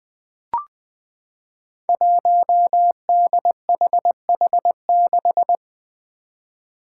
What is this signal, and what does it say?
Morse code, heard as a keyed tone, about 20 wpm, 700 Hz. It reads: 1DHH6
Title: Five Challenging Letter-Number Combinations 20wpm